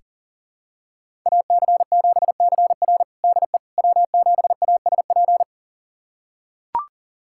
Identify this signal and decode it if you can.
Morse code, heard as a keyed tone, about 37 wpm, 700 Hz. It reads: AC7CR DE W7ASP
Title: USA Callsign DE Callsign - 5 Characters - 40wpm